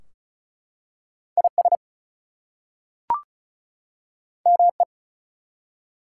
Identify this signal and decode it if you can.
Morse code, heard as a keyed tone, about 33 wpm, 700 Hz. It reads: IS ME
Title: Two Letter Words - 35wpm